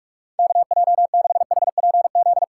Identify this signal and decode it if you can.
Morse code, heard as a keyed tone, about 44 wpm, 700 Hz. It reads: KJ6HPZ
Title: USA Call Signs - 6 Characters - 8x Character Spacing - 45wpm